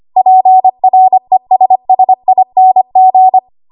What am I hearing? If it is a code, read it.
Morse code, heard as a keyed tone, about 25 wpm, 750 Hz. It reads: PRESSING
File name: Archive / 25 WPM / 25WPM